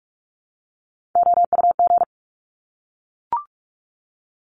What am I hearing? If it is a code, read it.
Morse code, heard as a keyed tone, about 46 wpm, 700 Hz. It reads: OUG
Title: Top Three Letter Combinations in Words 45wpm